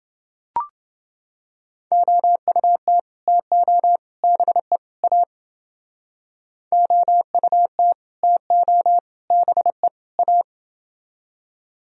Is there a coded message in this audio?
Yes: Morse code from 1.9 to 10.4 s, heard as a keyed tone, about 28 wpm, 700 Hz. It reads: OUT TO BE A OUT TO BE A